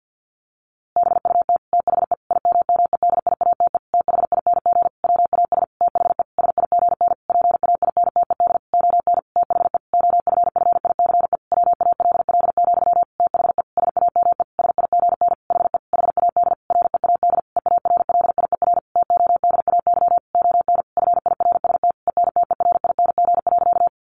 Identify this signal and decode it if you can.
Morse code, heard as a keyed tone, about 50 wpm, 700 Hz. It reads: BUT THE IMMEDIATE THING WAS THE SIGN PAINTED ON THE OFFICE WALL, THE SAME SIGN HE HAD READ EARLIER TODAY ON FIRST ENTERING.
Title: A Sound Of Thunder 50wpm